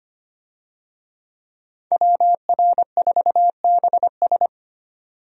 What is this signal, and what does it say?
Morse code, heard as a keyed tone, about 25 wpm, 700 Hz. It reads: WR4BS
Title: USA Call Signs - 5 Characters - 25wpm